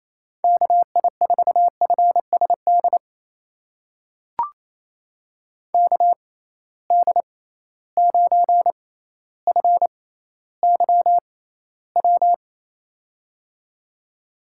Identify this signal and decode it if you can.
Morse code, heard as a keyed tone, about 28 wpm, spaced as for 14 wpm, 700 Hz. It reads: KI4FSD KD9FYW
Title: USA Call Signs - 6 Characters - 6x Character Spacing - 28wpm